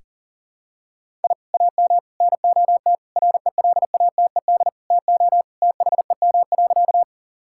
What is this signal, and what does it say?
Morse code, heard as a keyed tone, about 40 wpm, 700 Hz. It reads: I AM NOT RELATED TO THEM.
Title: Sentences from Top 1000 Words - Review Part 5 - 40wpm